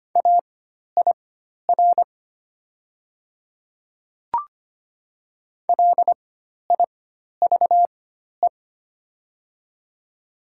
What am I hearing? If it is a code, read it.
Morse code, heard as a keyed tone, about 25 wpm, spaced as for 12 wpm, 700 Hz. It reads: AIR LIVE
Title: 4x Character Spacing - Top 200 Words - 25wpm